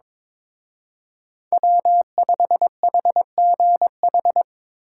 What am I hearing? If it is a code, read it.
Morse code, heard as a keyed tone, about 25 wpm, 700 Hz. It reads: W5HGH
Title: Speed Racing - USA 1x3 Call Signs 22wpm